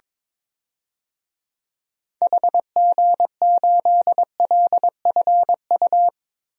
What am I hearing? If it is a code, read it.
Morse code, heard as a keyed tone, about 22 wpm, 700 Hz. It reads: HG8LFU
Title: DXCC-100 Call Signs - 6 Characters - 22wpm